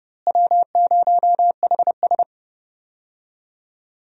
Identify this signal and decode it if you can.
Morse code, heard as a keyed tone, about 30 wpm, 700 Hz. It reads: W0HS
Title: USA Call Signs - 4 Characters - Repeated 2x - 30wpm